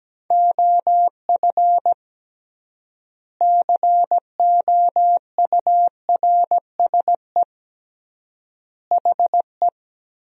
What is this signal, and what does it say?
Morse code, heard as a keyed tone, about 17 wpm, 700 Hz. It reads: OF COURSE HE
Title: 3x Word-Spacing - Sentences from Top 500 Words - 17wpm